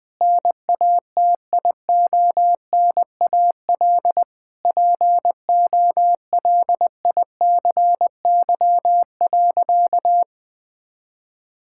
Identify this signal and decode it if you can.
Morse code, heard as a keyed tone, about 21 wpm, 700 Hz. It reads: NATIONAL POLICY.
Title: Sentences from Top 1000 Words - Review Part 6 - 20wpm